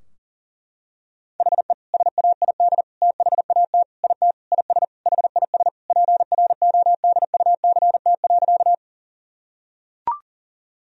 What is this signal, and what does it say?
Morse code, heard as a keyed tone, about 40 wpm, 700 Hz. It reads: HE SAID THAT IT IS HIS PRODUCT.
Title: Sentences from Top 1000 Words - Review Part 2 - 40wpm